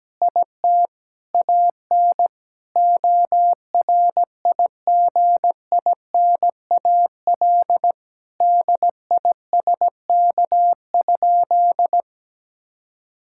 Morse code, heard as a keyed tone, about 17 wpm, 700 Hz. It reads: IT AN ORIGINAL DISK?
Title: Sentences from Top 1000 Words - Review Part 6 - 17wpm